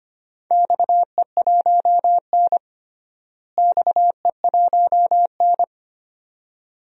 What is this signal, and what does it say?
Morse code, heard as a keyed tone, about 25 wpm, 700 Hz. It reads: XE1N XE1N
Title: DXCC-100 Call Signs - 4 Characters - Repeated 2x - 25wpm